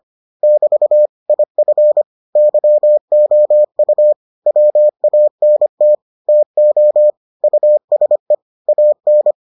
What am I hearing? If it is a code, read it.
Morse code, heard as a keyed tone, about 25 wpm, 600 Hz. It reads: <BT> IF YOU WANT TO USE AN